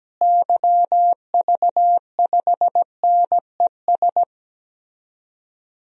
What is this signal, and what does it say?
Morse code, heard as a keyed tone, about 17 wpm, 700 Hz. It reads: YV5NES
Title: DXCC-100 Call Signs - 6 Characters - Repeated 2x - 17wpm